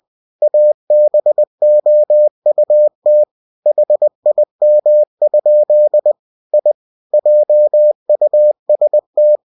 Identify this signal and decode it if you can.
Morse code, heard as a keyed tone, about 20 wpm, 600 Hz. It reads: ABOUT HIM? I JUST
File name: cw-books / winnie-the-pooh / winnie-20wpm-page-0005